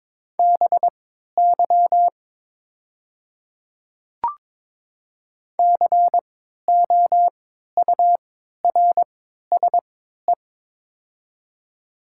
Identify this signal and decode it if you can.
Morse code, heard as a keyed tone, about 22 wpm, spaced as for 12 wpm, 700 Hz. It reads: BY COURSE
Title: 3x Character Spacing - Top 500 Words - 22wpm